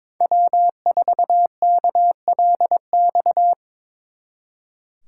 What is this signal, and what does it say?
Morse code, heard as a keyed tone, about 26 wpm, 700 Hz. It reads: W4KLX
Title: Speed Racing - USA 1x3 Call Signs 22wpm